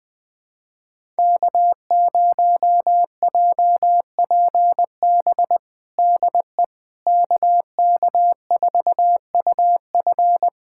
Morse code, heard as a keyed tone, about 20 wpm, 700 Hz. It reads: K0JPB DE KK4UF
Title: USA Callsign DE Callsign - 5 Characters - 20wpm